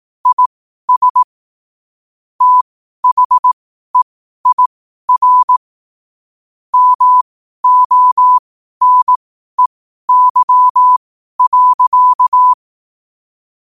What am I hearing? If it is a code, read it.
Morse code, heard as a keyed tone, about 18 wpm, spaced as for 12 wpm, 1000 Hz. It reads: IS THEIR MONEY.
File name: qotdfiles / 12wpm